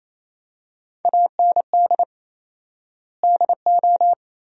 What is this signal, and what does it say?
Morse code, heard as a keyed tone, about 28 wpm, 700 Hz. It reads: AND DO
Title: 4x Word-Spacing - Sentences from Top 200 Words - 28wpm